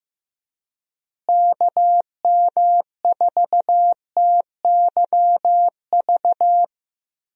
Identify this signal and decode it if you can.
Morse code, heard as a keyed tone, about 15 wpm, 700 Hz. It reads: KM4TYV
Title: USA Call Signs - 6 Characters - Repeated 3x - 15wpm